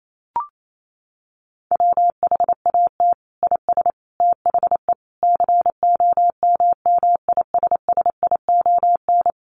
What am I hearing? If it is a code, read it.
Morse code, heard as a keyed tone, about 28 wpm, 700 Hz. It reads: WHAT IS THE COMMISSION
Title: Sentences from Top 1400 Words 28wpm